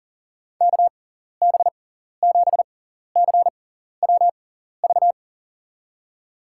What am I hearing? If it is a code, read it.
Morse code, heard as a keyed tone, about 40 wpm, spaced as for 14 wpm, 700 Hz. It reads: KB7CWV
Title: USA Call Signs - 6 Characters - 6x Character Spacing - 40wpm